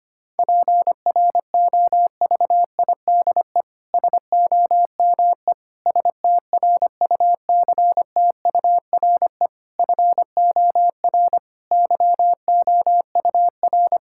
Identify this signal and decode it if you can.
Morse code, heard as a keyed tone, about 25 wpm, 700 Hz. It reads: PROVIDE SOME STRUCTURE FOR YOUR
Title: Sentences from Top 1000 Words - Review Part 6 - 25wpm